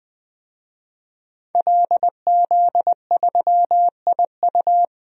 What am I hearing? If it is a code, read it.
Morse code, heard as a keyed tone, about 20 wpm, 700 Hz. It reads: LZ3IU
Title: DXCC-100 Call Signs - 5 Characters - 20wpm